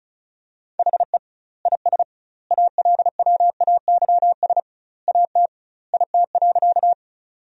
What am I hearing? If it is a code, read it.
Morse code, heard as a keyed tone, about 36 wpm, 700 Hz. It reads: HE IS ALWAYS AT IT.
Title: 2x Word-Spacing - Sentences from Top 300 Words - 35wpm